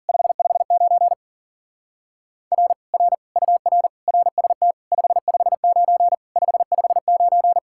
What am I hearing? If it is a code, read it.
Morse code, heard as a keyed tone, about 48 wpm, 700 Hz. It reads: 559 R R UR RST 559 559
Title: Speed-Racing Common QSO Phrases - 40wpm